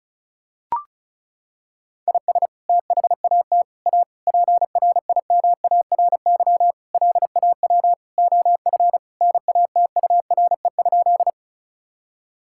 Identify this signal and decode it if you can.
Morse code, heard as a keyed tone, about 36 wpm, 700 Hz. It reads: IS THAT A PRIMARY LAW OF NATURE?
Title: Sentences from Top 1000 Words - Review Part 5 - 35wpm